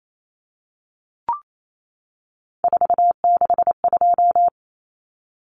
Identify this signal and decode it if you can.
Morse code, heard as a keyed tone, about 28 wpm, 700 Hz. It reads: 462
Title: Common Cut Numbers - 3-Digit 28wpm